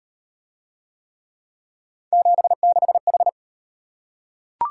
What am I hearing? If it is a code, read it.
Morse code, heard as a keyed tone, about 37 wpm, 700 Hz. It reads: 76H